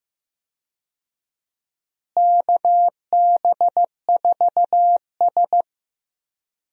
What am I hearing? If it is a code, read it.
Morse code, heard as a keyed tone, about 15 wpm, 700 Hz. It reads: KB4S
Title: USA Call Signs - 4 Characters - Repeated 2x - 15wpm